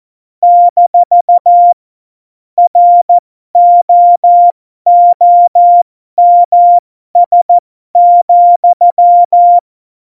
Morse code, heard as a keyed tone, about 14 wpm, spaced as for 12 wpm, 700 Hz. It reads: - ROOMS,
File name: Morsekurs / Witze